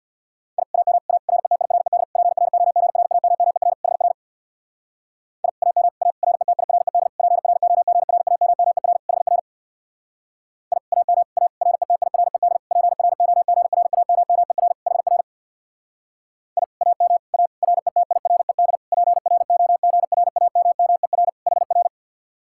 Morse code, heard as a keyed tone, about 61 wpm, 700 Hz. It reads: I AM A RETIRED PROGRAMMER HR I AM A RETIRED PROGRAMMER HR I AM A RETIRED PROGRAMMER HR I AM A RETIRED PROGRAMMER HR
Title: Speed-Racing Common QSO Phrases - 50wpm